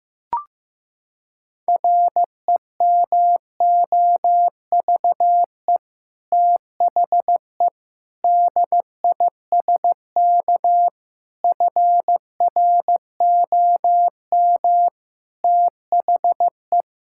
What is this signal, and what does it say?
Morse code, heard as a keyed tone, about 16 wpm, 700 Hz. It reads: REMOVE THE DISK FROM THE
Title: Sentences from Top 1000 Words - Review Part 6 - 15wpm